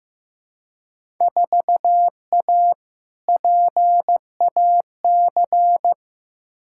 Morse code, heard as a keyed tone, about 15 wpm, 700 Hz. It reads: 4A PAC
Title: Field Day Exchanges Encore - 15wpm